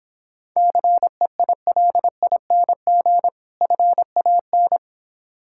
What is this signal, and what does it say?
Morse code, heard as a keyed tone, about 26 wpm, 700 Hz. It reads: CEILING FAN